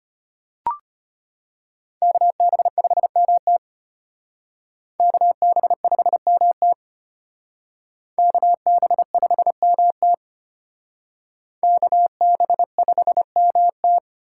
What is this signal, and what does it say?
Morse code, heard as a keyed tone, about 32 wpm, 700 Hz. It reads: KB5MT KB5MT KB5MT KB5MT